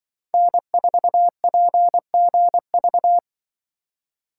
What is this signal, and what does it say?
Morse code, heard as a keyed tone, about 24 wpm, 700 Hz. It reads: N4PGV
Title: Speed Racing - USA 1x3 Call Signs 20wpm